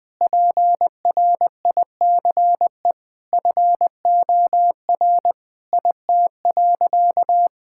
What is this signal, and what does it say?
Morse code, heard as a keyed tone, about 19 wpm, 700 Hz. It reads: PRICE FOR IT.
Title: Sentences from Top 1000 Words - Review Part 6 - 20wpm